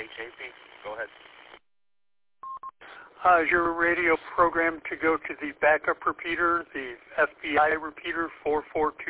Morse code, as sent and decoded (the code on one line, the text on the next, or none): none